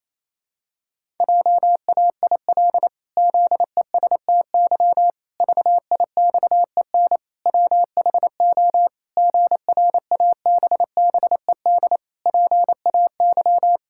Morse code, heard as a keyed tone, about 28 wpm, 700 Hz. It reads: JAIL ZESTY VIXEN WHO GRABBED PAY
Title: Pangram Challenge v2 - 28wpm